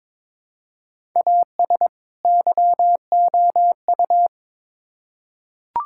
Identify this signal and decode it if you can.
Morse code, heard as a keyed tone, about 22 wpm, 700 Hz. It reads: AS YOU
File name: Build A Phrase - 2-Word Phrases - 22wpm